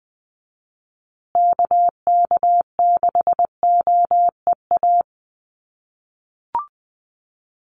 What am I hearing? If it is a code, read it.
Morse code, heard as a keyed tone, about 20 wpm, 700 Hz. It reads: KK6OEA
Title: USA Call Signs Repeated 2x 20wpm